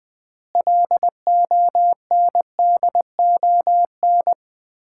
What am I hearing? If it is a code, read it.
Morse code, heard as a keyed tone, about 21 wpm, 700 Hz. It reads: LONDON
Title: Common QSO Phrases - 2x Word-Spacing - 20wpm